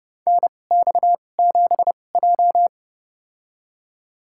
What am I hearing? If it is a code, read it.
Morse code, heard as a keyed tone, about 30 wpm, spaced as for 22 wpm, 700 Hz. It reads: NX7J
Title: USA Call Signs - 4 Characters - 2x Character Spacing - 30wpm